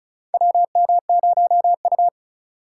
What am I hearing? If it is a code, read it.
Morse code, heard as a keyed tone, about 36 wpm, 700 Hz. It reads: WM0U
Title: USA Call Signs - 4 Characters - 2x Character Spacing - 35wpm